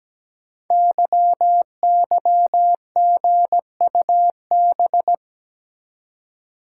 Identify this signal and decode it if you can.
Morse code, heard as a keyed tone, about 17 wpm, 700 Hz. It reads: YYGUB
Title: Code Groups - 5 Letters - 17wpm